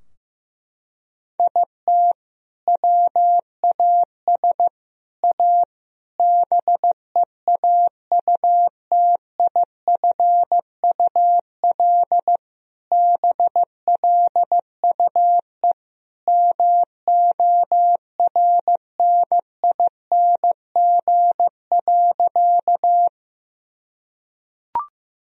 Morse code, heard as a keyed tone, about 15 wpm, 700 Hz. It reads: IT WAS A BEAUTIFUL BLUE MORNING.
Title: Sentences from Top 1000 Words - Review Part 3 - 15wpm